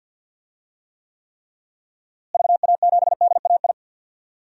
Morse code, heard as a keyed tone, about 47 wpm, 700 Hz. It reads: VA7DAI
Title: DXCC-100 Call Signs - 6 Characters - Repeated 3x - 50wpm